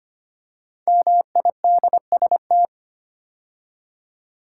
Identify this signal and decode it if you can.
Morse code, heard as a keyed tone, about 25 wpm, 700 Hz. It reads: MIDST
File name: 5 Letter Words - First 3 Letters Given - 25wpm